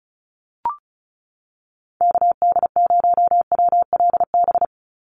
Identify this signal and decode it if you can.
Morse code, heard as a keyed tone, about 36 wpm, 700 Hz. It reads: KD0WLB
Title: USA Call Signs 35wpm